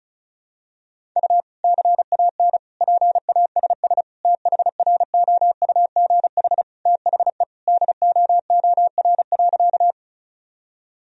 Morse code, heard as a keyed tone, about 36 wpm, 700 Hz. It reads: U CAN PASS THROUGH THE DOOR.
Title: Sentences from Top 400 Words and Abbreviations 35wpm